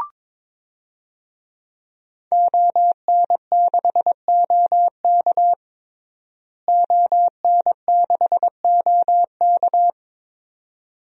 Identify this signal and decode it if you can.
Morse code, heard as a keyed tone, about 22 wpm, 700 Hz. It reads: ON6OK ON6OK
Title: DXCC-100 Call Signs - 5 Characters - Repeated 2x - 22wpm